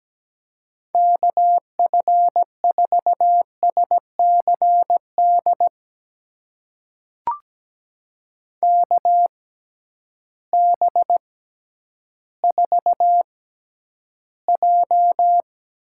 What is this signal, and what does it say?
Morse code, heard as a keyed tone, about 17 wpm, spaced as for 10 wpm, 700 Hz. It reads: KF4SCD KB4J
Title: USA Call Signs - 6 Characters - 6x Character Spacing - 17wpm